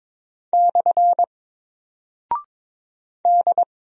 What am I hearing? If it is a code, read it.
Morse code, heard as a keyed tone, about 22 wpm, 700 Hz. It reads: / D
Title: Single Character (Letter:Number:Punctuation:Prosign) Alt 22wpm